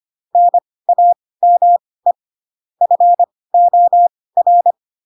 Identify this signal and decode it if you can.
Morse code, heard as a keyed tone, about 25 wpm, spaced as for 18 wpm, 700 Hz. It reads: NAME FOR
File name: Pinocchio ch3 12 25 ik1hge